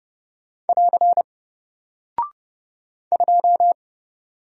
Morse code, heard as a keyed tone, about 30 wpm, 700 Hz. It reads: <AR> 2
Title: Single Character (Letter:Number:Punctuation:Prosign) Alt 30wpm